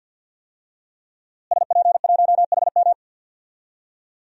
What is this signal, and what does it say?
Morse code, heard as a keyed tone, about 50 wpm, 700 Hz. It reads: SP1HM
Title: DXCC-100 Call Signs - 5 Characters - Repeated 2x - 50wpm